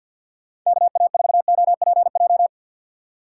Morse code, heard as a keyed tone, about 52 wpm, 700 Hz. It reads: KA4OPJ
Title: USA Call Signs - 6 Characters - 3x Character Spacing - 50wpm